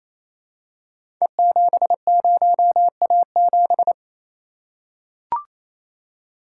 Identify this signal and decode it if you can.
Morse code, heard as a keyed tone, about 28 wpm, 700 Hz. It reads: E70A7
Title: Code Groups - 5 Letter:Numbers - 28wpm